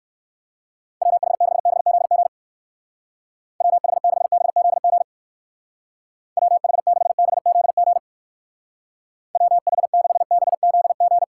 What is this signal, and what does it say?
Morse code, heard as a keyed tone, about 55 wpm, 700 Hz. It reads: WH6BZG WH6BZG WH6BZG WH6BZG
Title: Speed Racing - USA Call Signs 45wpm